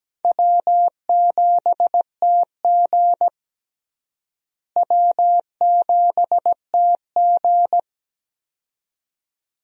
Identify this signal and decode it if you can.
Morse code, heard as a keyed tone, about 17 wpm, 700 Hz. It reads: W7TG W7TG
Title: USA Call Signs - 4 Characters - Repeated 2x - 17wpm